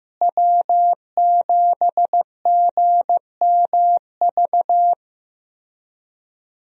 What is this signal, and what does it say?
Morse code, heard as a keyed tone, about 15 wpm, 700 Hz. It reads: W7GMV
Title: USA Call Signs - 5 Characters - Repeated 2x - 15wpm